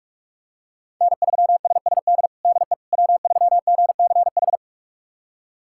Morse code, heard as a keyed tone, about 46 wpm, 700 Hz. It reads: N3SSD DE W3GCH
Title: USA Callsign DE Callsign - 5 Characters - 45wpm